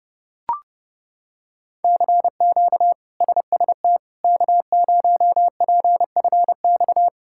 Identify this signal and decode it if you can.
Morse code, heard as a keyed tone, about 31 wpm, 700 Hz. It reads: CQ SST K0PFX
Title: K1USN SST QSO Practice - 30wpm